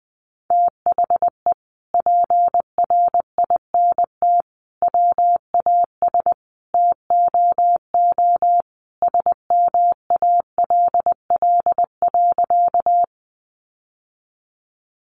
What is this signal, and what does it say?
Morse code, heard as a keyed tone, about 20 wpm, 700 Hz. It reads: THE PRINT WAS TOO SMALL.
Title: Sentences from Top 1400 Words 20wpm